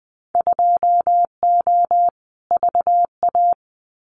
Morse code, heard as a keyed tone, about 20 wpm, 700 Hz. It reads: 2O VA
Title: Winter Field Day Exchanges 20wpm